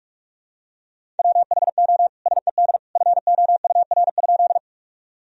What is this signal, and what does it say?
Morse code, heard as a keyed tone, about 46 wpm, 700 Hz. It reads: WHO SED FOUR?
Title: Sentences from Top 300 Words and Abbreviations 45wpm